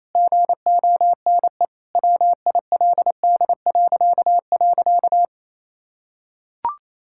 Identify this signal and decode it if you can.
Morse code, heard as a keyed tone, about 28 wpm, 700 Hz. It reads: GONE WILD..
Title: Pangram Challenge v2 - 28wpm